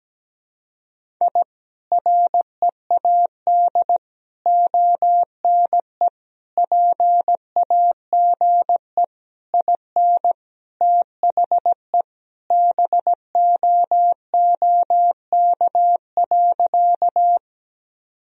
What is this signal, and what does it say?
Morse code, heard as a keyed tone, about 17 wpm, 700 Hz. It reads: I READ ONE PAGE IN THE BOOK.
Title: Sentences from Top 1000 Words - Review Part 1 - 17wpm